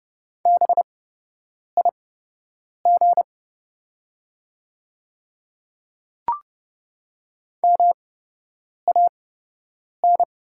Morse code, heard as a keyed tone, about 30 wpm, spaced as for 8 wpm, 700 Hz. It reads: BIG MAN
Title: 8x Character Spacing - Top 200 Words - 30wpm